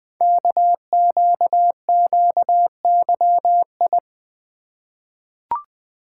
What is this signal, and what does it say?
Morse code, heard as a keyed tone, about 20 wpm, 700 Hz. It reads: KQQYI
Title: Code Groups - 5 Letter:Numbers - 20wpm